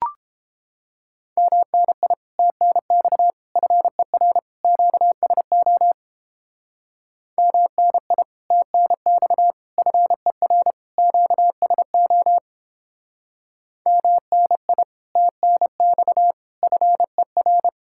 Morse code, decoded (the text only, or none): MNI TNX FER QSO MNI TNX FER QSO MNI TNX FER